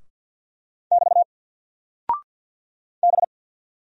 Morse code, heard as a keyed tone, about 48 wpm, 700 Hz. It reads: <BT> B
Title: Single Character (Letter:Number:Punctuation:Prosign) Alt 50wpm